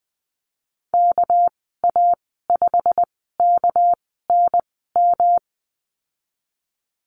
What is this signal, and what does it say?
Morse code, heard as a keyed tone, about 19 wpm, spaced as for 14 wpm, 700 Hz. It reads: KA5KNM
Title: 200 Lesson 32g - Letter k - Call Signs 20wpm